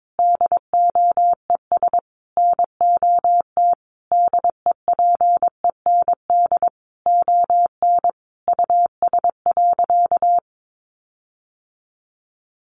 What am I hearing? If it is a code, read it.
Morse code, heard as a keyed tone, about 22 wpm, 700 Hz. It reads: DOES NOT DEPEND ON US.
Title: Sentences from Top 1300 Words 22wpm